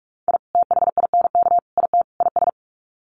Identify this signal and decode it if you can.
Morse code, heard as a keyed tone, about 45 wpm, 700 Hz. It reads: I THINK IT IS
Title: Sets of 4 Words 45wpm